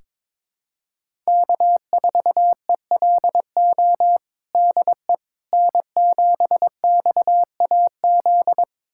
Morse code, heard as a keyed tone, about 22 wpm, 700 Hz. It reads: K4ELO DE N7XAZ
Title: USA Callsign DE Callsign 22wpm